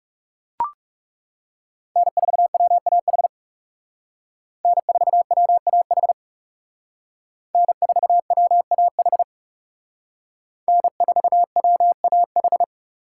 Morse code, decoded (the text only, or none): N4WAH N4WAH N4WAH N4WAH